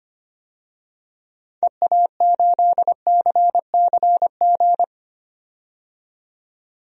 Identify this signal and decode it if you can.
Morse code, heard as a keyed tone, about 25 wpm, 700 Hz. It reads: EA8CCG
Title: DXCC-100 Call Signs - 6 Characters - 25wpm